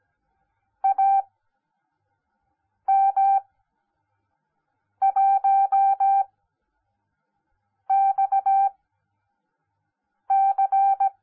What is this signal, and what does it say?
Morse code, heard as a keyed tone, about 17 wpm, spaced as for 5 wpm, 780 Hz. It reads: AM1XC
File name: Onlinelehrgang / morsen / DJ4UF Morsen10